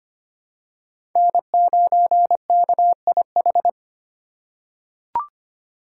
Morse code, heard as a keyed tone, about 25 wpm, 700 Hz. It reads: N9KIH
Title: USA Call Signs - 5 Characters - 4x Character Spacing - 25wpm